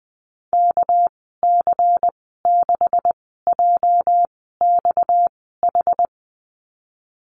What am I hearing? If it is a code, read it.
Morse code, heard as a keyed tone, about 19 wpm, spaced as for 14 wpm, 700 Hz. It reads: KC6JXH
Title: 231 Lesson 37g - Letter x - Call Signs 20wpm